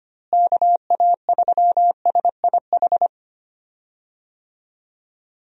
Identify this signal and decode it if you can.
Morse code, heard as a keyed tone, about 25 wpm, 700 Hz. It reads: KA3SIH
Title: USA Call Signs - 6 Characters - Repeated 3x - 25wpm